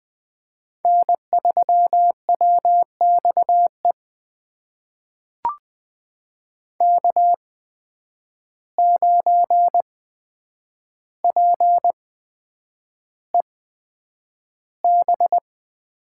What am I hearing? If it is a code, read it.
Morse code, heard as a keyed tone, about 20 wpm, spaced as for 9 wpm, 700 Hz. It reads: N3WXE K9PEB
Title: USA Call Signs - 5 Characters - 8x Character Spacing - 20wpm